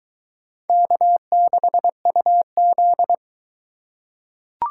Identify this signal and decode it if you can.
Morse code, heard as a keyed tone, about 21 wpm, 700 Hz. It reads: K6UZ